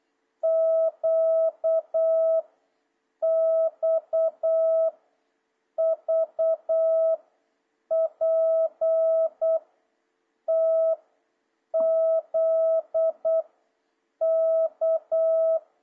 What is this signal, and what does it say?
Morse code, heard as a keyed tone, about 8 wpm, 650 Hz. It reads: Q X V P T Z K
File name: jh8cbh / 2026cw